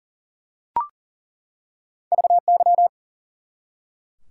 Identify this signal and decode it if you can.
Morse code, heard as a keyed tone, about 40 wpm, 700 Hz. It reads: VY
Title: Top 90 QSO Elements 40wpm